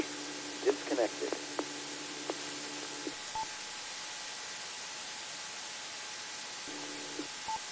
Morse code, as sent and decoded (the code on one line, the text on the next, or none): none